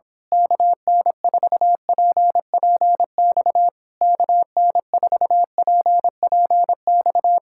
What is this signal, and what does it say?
Morse code, heard as a keyed tone, about 27 wpm, 700 Hz. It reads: KN4PPX KN4PPX
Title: Speed-Racing Common QSO Phrases Encore - 22wpm